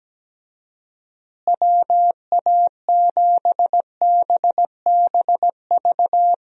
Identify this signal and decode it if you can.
Morse code, heard as a keyed tone, about 17 wpm, 700 Hz. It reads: WA7BBV
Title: USA Call Signs - 6 Characters - Repeated 3x - 17wpm